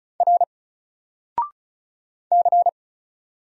Morse code, heard as a keyed tone, about 36 wpm, 700 Hz. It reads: R C
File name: Single-Letter-Number-35wpm